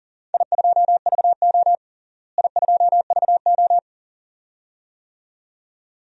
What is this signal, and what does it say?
Morse code, heard as a keyed tone, about 40 wpm, 690 Hz. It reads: I2VO I2VO
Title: DXCC-100 Call Signs - 4 Characters - Repeated 2x - 40wpm